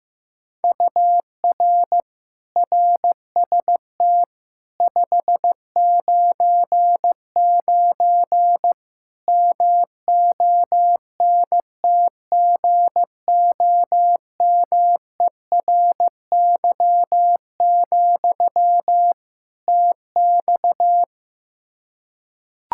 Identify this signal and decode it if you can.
Morse code, heard as a keyed tone, about 15 wpm, 700 Hz. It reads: UR RST 599 MONTGOMERY, TX
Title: K1USN SST QSO Practice 15wpm